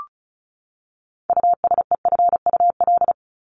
Morse code, heard as a keyed tone, about 34 wpm, 700 Hz. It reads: USEFUL
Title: Top 501-750 Words 35wpm